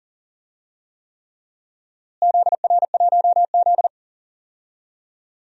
Morse code, heard as a keyed tone, about 39 wpm, 700 Hz. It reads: ZR1Z
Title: DXCC-100 Call Signs - 4 Characters - Repeated 3x - 40wpm